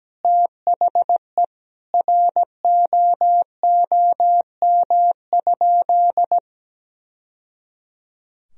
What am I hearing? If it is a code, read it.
Morse code, heard as a keyed tone, about 17 wpm, 700 Hz. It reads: THE ROOM?
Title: Sentences from Top 1000 Words - Review Part 4 - 17wpm